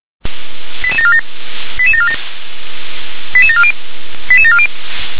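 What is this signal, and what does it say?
Five-tone selective call (ZVEI), heard as quick runs of five tones, tones of about 70 ms. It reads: 88747 88747 88740 88740
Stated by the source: ZVEI five-tone selective call 88912 received by SDR.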